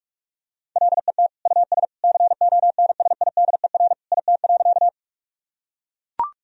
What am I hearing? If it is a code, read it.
Morse code, heard as a keyed tone, about 42 wpm, 700 Hz. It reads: LET US CONSIDER IT.
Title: Sentences from Top 1000 Words - Review Part 4 - 45wpm